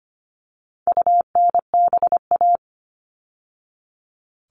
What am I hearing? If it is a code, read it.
Morse code, heard as a keyed tone, about 25 wpm, 700 Hz. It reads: UNBA
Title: All Cut Numbers - 4-Digit 25wpm